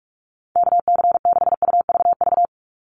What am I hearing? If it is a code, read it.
Morse code, heard as a keyed tone, about 44 wpm, 700 Hz. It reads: KC6UVV
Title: USA Call Signs 45wpm